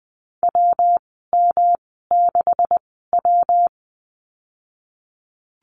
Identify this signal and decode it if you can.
Morse code, heard as a keyed tone, about 20 wpm, spaced as for 15 wpm, 700 Hz. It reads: WM6W
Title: 128 Lesson 21g - Number 6 - Call Signs 20wpm